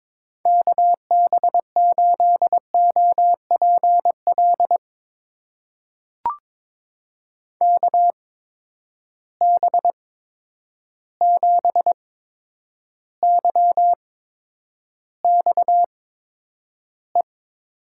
Morse code, heard as a keyed tone, about 22 wpm, spaced as for 9 wpm, 700 Hz. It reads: KB8OPL KB7YXE
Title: USA Call Signs - 8x Character Spacing - 22wpm